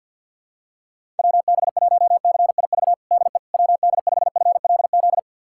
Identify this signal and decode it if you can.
Morse code, heard as a keyed tone, about 52 wpm, 700 Hz. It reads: WB1CIV DE WD5FLZ
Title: USA Callsign DE Callsign - 6 Characters - 50wpm